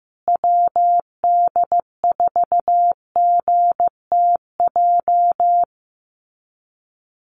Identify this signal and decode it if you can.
Morse code, heard as a keyed tone, about 15 wpm, 700 Hz. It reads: WD4GTJ
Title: USA Call Signs Repeated 2x 15wpm